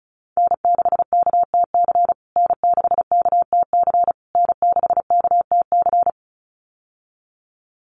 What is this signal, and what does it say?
Morse code, heard as a keyed tone, about 34 wpm, 700 Hz. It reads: N6KTC N6KTC N6KTC
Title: USA Call Signs Repeated 3x 35wpm